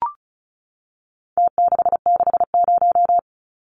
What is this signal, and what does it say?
Morse code, heard as a keyed tone, about 34 wpm, 700 Hz. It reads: T660
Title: Numbers - Contest Style - 4-Digit 35wpm